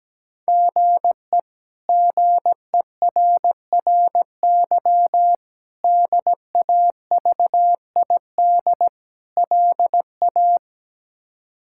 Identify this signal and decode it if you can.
Morse code, heard as a keyed tone, about 17 wpm, 700 Hz. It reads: GE GERRY DAVID LA
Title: K1USN SST QSO Practice - 17wpm